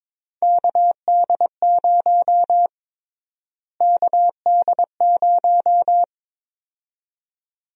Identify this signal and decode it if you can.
Morse code, heard as a keyed tone, about 22 wpm, 700 Hz. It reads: KD0 KD0
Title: USA Call Sign Prefixes - Repeated 2x - 22wpm